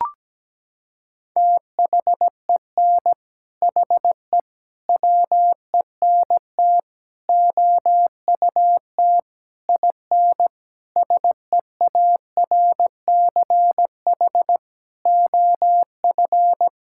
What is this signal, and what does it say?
Morse code, heard as a keyed tone, about 17 wpm, 700 Hz. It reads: THEN HE WENT OUT IN SEARCH OF
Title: Sentences from Top 1000 Words - Review Part 5 - 17wpm